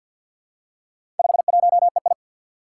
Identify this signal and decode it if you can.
Morse code, heard as a keyed tone, about 48 wpm, 700 Hz. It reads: 51EI
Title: Four Challenging Letter-Number Combinations 50wpm